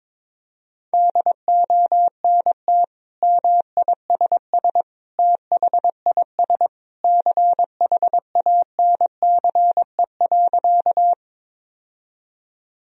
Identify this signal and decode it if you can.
Morse code, heard as a keyed tone, about 21 wpm, 700 Hz. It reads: DONT MISS THIS CHANCE.
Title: Sentences from Top 1000 Words - Review Part 3 - 22wpm